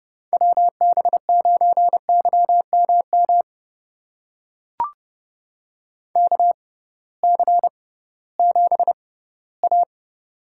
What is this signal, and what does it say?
Morse code, heard as a keyed tone, about 32 wpm, spaced as for 18 wpm, 700 Hz. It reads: WB9YMM KC7A
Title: USA Call Signs - 6 Characters - 6x Character Spacing - 30wpm